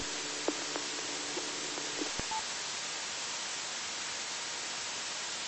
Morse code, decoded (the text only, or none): none